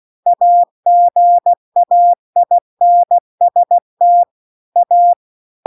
Morse code, heard as a keyed tone, about 16 wpm, 700 Hz. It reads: AGAINST A
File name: CW-The-Big-Twoheaded-River-2-16WPM